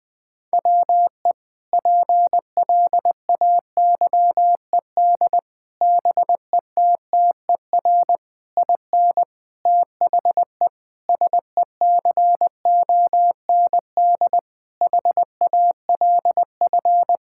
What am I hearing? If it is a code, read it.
Morse code, heard as a keyed tone, about 21 wpm, 700 Hz. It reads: WE PLAYED BETTER IN THE SECOND HALF
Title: Sentences with 5-Word Phrases - 20wpm